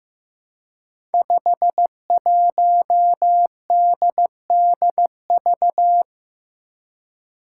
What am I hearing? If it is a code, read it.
Morse code, heard as a keyed tone, about 15 wpm, 700 Hz. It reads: 51DDV
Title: Five Challenging Letter-Number Combinations 15wpm